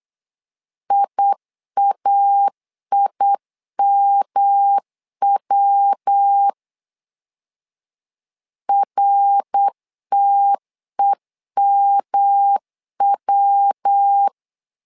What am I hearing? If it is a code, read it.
Morse code, heard as a keyed tone, about 8 wpm, 800 Hz. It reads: IAIMW RTEMW
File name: LEZIONI / 3 MWRO / 3 MWR 8wpm 2444